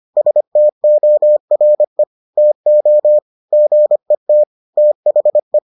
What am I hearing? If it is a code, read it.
Morse code, heard as a keyed tone, about 25 wpm, 600 Hz. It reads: STORE TO GET THE